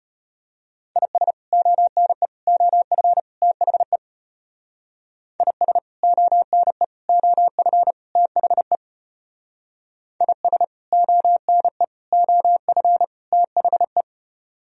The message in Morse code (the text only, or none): IS ONE OF THE IS ONE OF THE IS ONE OF THE